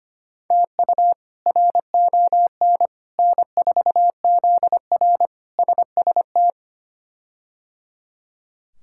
Morse code, heard as a keyed tone, about 25 wpm, 700 Hz. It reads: TU RON N4ZR SST
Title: K1USN SST QSO Practice - 25wpm